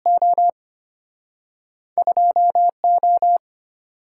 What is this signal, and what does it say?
Morse code, heard as a keyed tone, about 34 wpm, spaced as for 30 wpm, 700 Hz. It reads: O 2O
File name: practice / 25wpm / Speed-Racing - Two Letter-Number Combinations - 25wpm